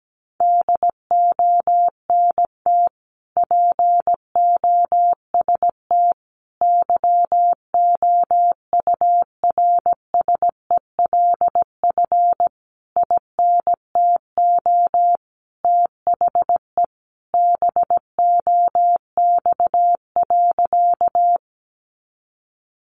Morse code, heard as a keyed tone, about 17 wpm, 700 Hz. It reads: DONT POST YOURSELF INTO THE BOX.
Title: Sentences from Top 1000 Words 17wpm